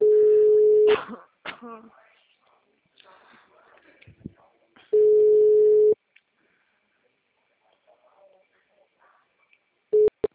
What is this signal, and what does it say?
Ringback tone (Europe, 425 Hz), with speech.